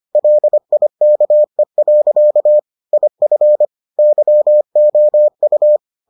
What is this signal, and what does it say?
Morse code, heard as a keyed tone, about 25 wpm, 600 Hz. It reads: LIKE. IF YOU